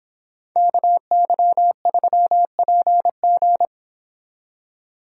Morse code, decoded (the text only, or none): KY3PG